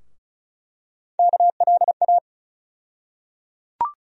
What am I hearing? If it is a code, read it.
Morse code, heard as a keyed tone, about 34 wpm, 700 Hz. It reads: KLA
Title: USA Call Sign Postfixes - 3 Letters 35wpm